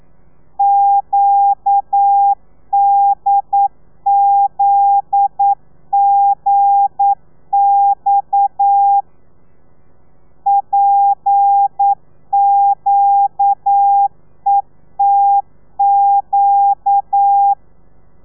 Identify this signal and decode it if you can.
Morse code, heard as a keyed tone, about 9 wpm, 800 Hz. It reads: QDZGX PQETQ
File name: telegrafia / i2tfj / Lez9 40